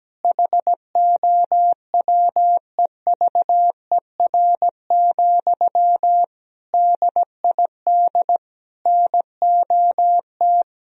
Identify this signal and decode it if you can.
Morse code, heard as a keyed tone, about 17 wpm, 700 Hz. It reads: HOWEVER, DID NOT
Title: Sentences from Top 1000 Words - Review Part 3 - 17wpm